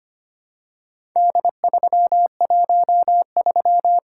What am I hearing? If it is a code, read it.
Morse code, heard as a keyed tone, about 25 wpm, 700 Hz. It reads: D313
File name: practice / 25wpm / Four Letter-Number Combinations - 25wpm